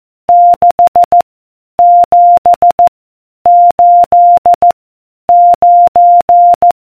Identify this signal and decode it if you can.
Morse code, heard as a keyed tone, about 14 wpm, 700 Hz. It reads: 6 7 8 9